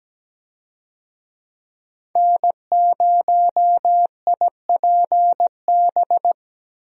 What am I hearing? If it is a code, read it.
Morse code, heard as a keyed tone, about 17 wpm, 700 Hz. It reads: N0IPB
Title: USA Call Signs - 5 Characters - Repeated 2x - 17wpm